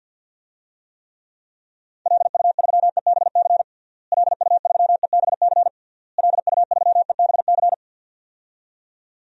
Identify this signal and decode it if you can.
Morse code, heard as a keyed tone, about 49 wpm, 700 Hz. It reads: LU3EBC LU3EBC LU3EBC
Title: DXCC-100 Call Signs - 6 Characters - Repeated 3x - 50wpm